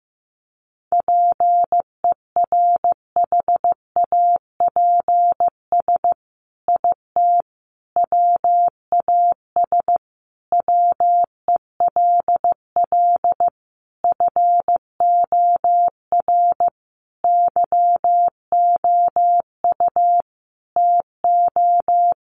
Morse code, heard as a keyed tone, about 15 wpm, 700 Hz. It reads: PERHAPS IT WAS WELL FOR YOU TO
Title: Sentences from Top 1400 Words 15wpm